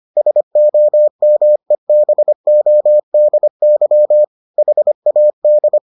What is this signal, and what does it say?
Morse code, heard as a keyed tone, about 25 wpm, 600 Hz. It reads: SOMEBODY HAD